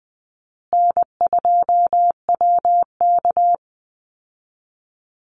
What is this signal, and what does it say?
Morse code, heard as a keyed tone, about 19 wpm, 700 Hz. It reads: N2WK
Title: Easy Call Signs Repeated 2x - 20wpm